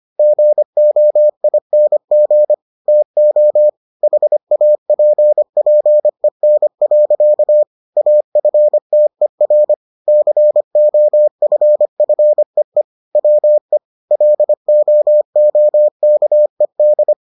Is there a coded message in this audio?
Yes: Morse code from 0.2 to 17.1 s, heard as a keyed tone, about 25 wpm, 600 Hz. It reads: GOING TO HAPPEN. AFTER COFFEE WE LOOKED